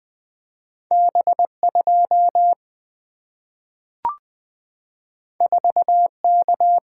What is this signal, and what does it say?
Morse code, heard as a keyed tone, about 20 wpm, 700 Hz. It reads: B 2 4 K
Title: Single-Letter Single-Number 20wpm